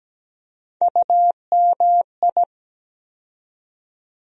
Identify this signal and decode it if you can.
Morse code, heard as a keyed tone, about 17 wpm, 700 Hz. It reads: UMI
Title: USA Call Sign Postfixes - 3 Letters 17wpm